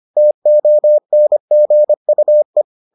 Morse code, heard as a keyed tone, about 25 wpm, 600 Hz. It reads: TONGUE